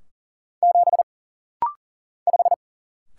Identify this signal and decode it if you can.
Morse code, heard as a keyed tone, about 40 wpm, 700 Hz. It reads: 7 5
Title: Single Letter-Number - Rapid-Fire 40wpm